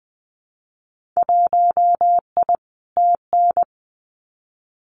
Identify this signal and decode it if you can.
Morse code, heard as a keyed tone, about 20 wpm, 700 Hz. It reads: 1I TN
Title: Winter Field Day Exchanges 20wpm